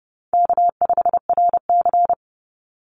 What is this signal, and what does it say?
Morse code, heard as a keyed tone, about 30 wpm, 700 Hz. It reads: K5RC
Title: Call Signs - 30wpm